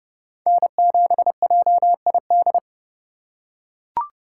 Morse code, heard as a keyed tone, about 30 wpm, 700 Hz. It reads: N7JID
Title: USA Call Signs - 4x Character Spacing - 30wpm